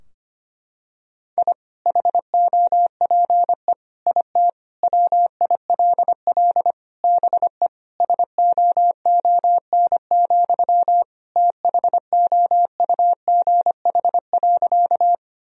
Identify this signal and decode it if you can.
Morse code, heard as a keyed tone, about 25 wpm, 700 Hz. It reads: I HOPE IT WILL BE SOON, THOUGH.
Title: Sentences from Top 1000 Words - Review Part 2 - 25wpm